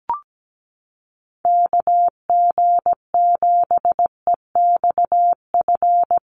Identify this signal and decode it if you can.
Morse code, heard as a keyed tone, about 17 wpm, 700 Hz. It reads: KG7EXF
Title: USA Call Signs Repeated 3x 17wpm